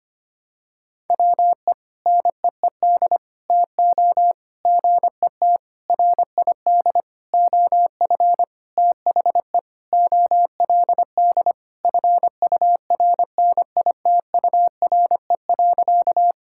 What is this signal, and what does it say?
Morse code, heard as a keyed tone, about 25 wpm, 700 Hz. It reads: WE NEED TO GET RID OF THE OLD FURNITURE.
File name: practice / 25wpm / Sentences with 4-Word Phrases - 25wpm